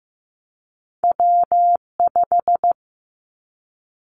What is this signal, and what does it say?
Morse code, heard as a keyed tone, about 15 wpm, 700 Hz. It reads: W5
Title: Callsign Prefixes for Top 25 DXCC - 15wpm